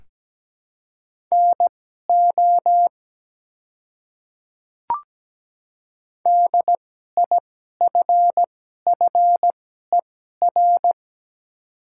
Morse code, heard as a keyed tone, about 17 wpm, spaced as for 12 wpm, 700 Hz. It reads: NO DIFFER
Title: 2x Character Spacing - Top 200 Words - 17wpm